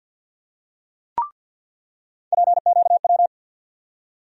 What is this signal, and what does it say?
Morse code, heard as a keyed tone, about 47 wpm, 700 Hz. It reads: PQW
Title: USA Call Sign Postfixes - 3 Letters 50wpm